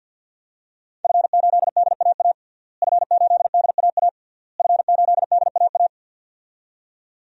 Morse code, heard as a keyed tone, about 48 wpm, 700 Hz. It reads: F8DAA F8DAA F8DAA
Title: DXCC-100 Call Signs - 5 Characters - Repeated 3x - 50wpm